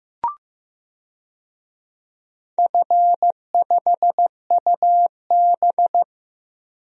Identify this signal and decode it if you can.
Morse code, heard as a keyed tone, about 16 wpm, 700 Hz. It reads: F5UB
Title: DXCC-100 Call Signs - 4 Characters - Repeated 3x - 15wpm